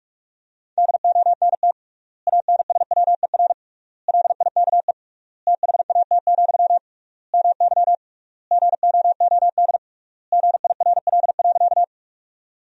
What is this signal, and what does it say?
Morse code, heard as a keyed tone, about 46 wpm, 700 Hz. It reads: DONT ANSWER LIKE THAT, MY GOOD GIRL.
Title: 3x Word-Spacing - Sentences from Top 400 Words - 45wpm